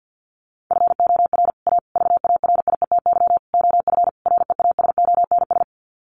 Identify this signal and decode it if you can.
Morse code, heard as a keyed tone, about 50 wpm, 700 Hz. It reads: FOR A VARIETY OF REASONS
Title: Sets of 5 Words Encore 50wpm